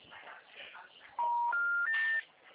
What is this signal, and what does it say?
Special information tone.
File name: out- -2005- - - .17783